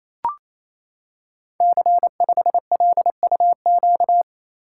Morse code, heard as a keyed tone, about 28 wpm, 700 Hz. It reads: C5LUQ
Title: Code Groups - 5 Letter:Numbers - 28wpm